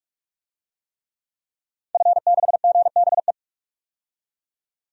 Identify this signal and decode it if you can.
Morse code, heard as a keyed tone, about 45 wpm, 700 Hz. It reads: F6GBE
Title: DXCC-100 Call Signs - 5 Characters - 45wpm